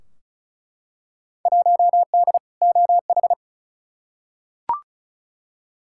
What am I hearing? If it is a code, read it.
Morse code, heard as a keyed tone, about 34 wpm, 700 Hz. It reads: 1D OH
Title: Field Day Exchanges - 35wpm